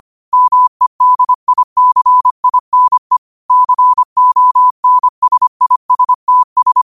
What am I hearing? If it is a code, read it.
Morse code, heard as a keyed tone, about 25 wpm, 1000 Hz. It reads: MEDICINE CONSISTS
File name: qotdfiles / 25wpm